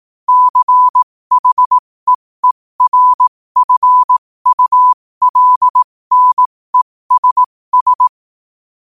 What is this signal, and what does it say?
Morse code, heard as a keyed tone, about 18 wpm, spaced as for 15 wpm, 1000 Hz. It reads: CHEERFULNESS
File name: qotdfiles / 15wpm